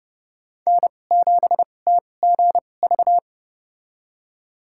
Morse code, heard as a keyed tone, about 29 wpm, spaced as for 21 wpm, 700 Hz. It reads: N7TGV
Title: USA Call Signs - 2x Character Spacing - 30wpm